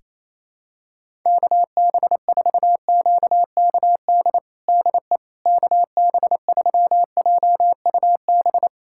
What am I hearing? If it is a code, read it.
Morse code, heard as a keyed tone, about 28 wpm, 700 Hz. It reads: KB4QKD DE KB3JUB
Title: USA Callsign DE Callsign - 6 Characters - 28wpm